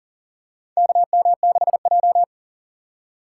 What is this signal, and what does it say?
Morse code, heard as a keyed tone, about 40 wpm, 700 Hz. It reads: KM6J
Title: USA Call Signs - 3x Character Spacing - 40wpm